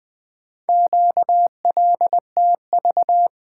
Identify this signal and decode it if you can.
Morse code, heard as a keyed tone, about 20 wpm, 700 Hz. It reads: QLTV
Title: Four Letter-Number Combinations - 20wpm